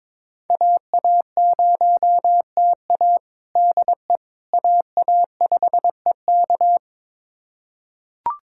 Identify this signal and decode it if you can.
Morse code, heard as a keyed tone, about 22 wpm, 700 Hz. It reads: AA0TA DE AA5EK
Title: USA Callsign DE Callsign - 5 Characters - 22wpm